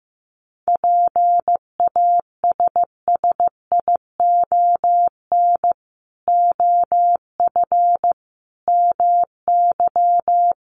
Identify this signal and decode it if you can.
Morse code, heard as a keyed tone, about 15 wpm, 700 Hz. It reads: PASSION OF MY
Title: Sentences from Top 1300 Words 15wpm